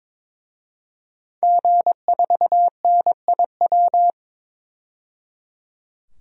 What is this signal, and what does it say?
Morse code, heard as a keyed tone, about 22 wpm, 700 Hz. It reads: G4NIW
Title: DXCC-100 Call Signs - 5 Characters - 22wpm